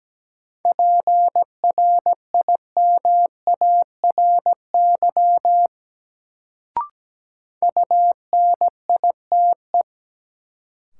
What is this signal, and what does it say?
Morse code, heard as a keyed tone, about 17 wpm, 700 Hz. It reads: PRIMARY UNITE
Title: Sentences from Top 1000 Words - Review Part 5 - New Word Drill - 17wpm